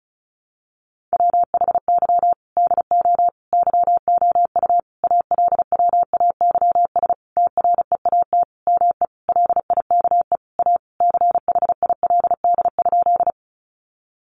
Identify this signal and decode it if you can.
Morse code, heard as a keyed tone, about 34 wpm, 700 Hz. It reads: WHY DO YOU ALWAYS TREAT ME LIKE A CHILD?
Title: Sentences from Top 1300 Words 35wpm